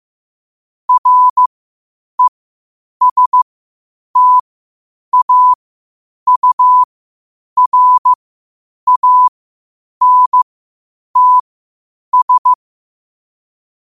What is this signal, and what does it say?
Morse code, heard as a keyed tone, about 15 wpm, spaced as for 8 wpm, 1000 Hz. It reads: RESTAURANTS
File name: qotdfiles / 8wpm